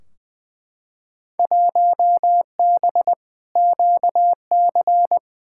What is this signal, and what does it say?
Morse code, heard as a keyed tone, about 20 wpm, 700 Hz. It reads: 1B QC
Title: Field Day Exchanges Encore - 20wpm